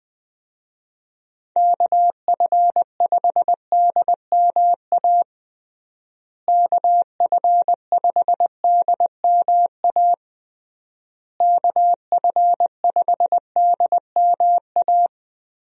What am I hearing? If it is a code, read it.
Morse code, heard as a keyed tone, about 20 wpm, 700 Hz. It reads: KF5DMA KF5DMA KF5DMA
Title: USA Call Signs - 6 Characters - Repeated 3x - 20wpm